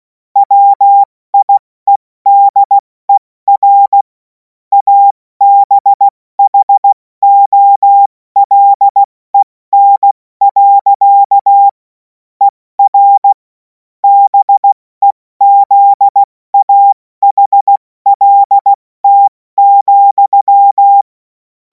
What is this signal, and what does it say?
Morse code, heard as a keyed tone, about 16 wpm, 800 Hz. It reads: WIEDER ABHOLEN. ER BEZAHLT,
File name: Morsekurs / Witze / Witz 28. April 2026 70 F80